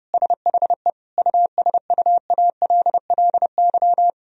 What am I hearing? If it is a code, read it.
Morse code, heard as a keyed tone, about 32 wpm, 700 Hz. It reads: SHE USUALLY
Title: Sentences from Top 1000 Words - Review Part 3 - 30wpm